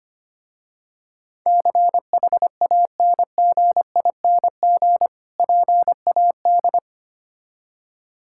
Morse code, heard as a keyed tone, about 25 wpm, 700 Hz. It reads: CHANGING PAD
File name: practice / 25wpm / Difficult Words with Hints - 25wpm